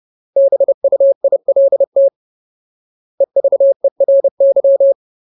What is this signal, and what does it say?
Morse code, heard as a keyed tone, about 30 wpm, 550 Hz. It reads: BUILT EVERY
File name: Ultimate Word List / 30WPM Ultimate